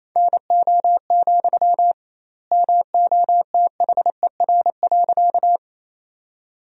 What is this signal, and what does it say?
Morse code, heard as a keyed tone, about 27 wpm, 700 Hz. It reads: NO, MOTHER.
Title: 2x Word-Spacing - Sentences from Top 200 Words - 28wpm